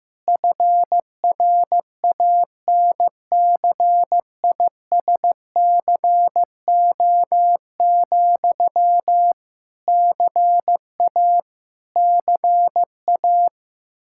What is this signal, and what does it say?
Morse code, heard as a keyed tone, about 15 wpm, 700 Hz. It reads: FRANCISCO, CA CA
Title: Practice QSO - Exchange 1 - Beginning - 15wpm